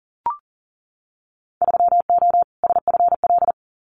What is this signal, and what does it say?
Morse code, heard as a keyed tone, about 40 wpm, 700 Hz. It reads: 3O SFL
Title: Winter Field Day Exchanges 40wpm